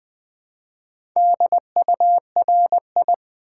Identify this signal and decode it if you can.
Morse code, heard as a keyed tone, about 20 wpm, 700 Hz. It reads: DURI
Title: Build A Word - 2 Letters At A Time - 6-Letter Words - 20wpm